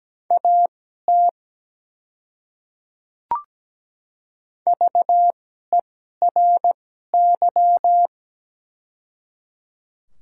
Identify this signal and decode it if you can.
Morse code, heard as a keyed tone, about 17 wpm, spaced as for 12 wpm, 700 Hz. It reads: AT VERY
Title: 2x Character Spacing - Top 200 Words - 17wpm